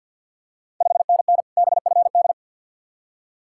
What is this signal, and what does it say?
Morse code, heard as a keyed tone, about 51 wpm, 690 Hz. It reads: 5NN BFD
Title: Ontario QSO Party Exchange 50wpm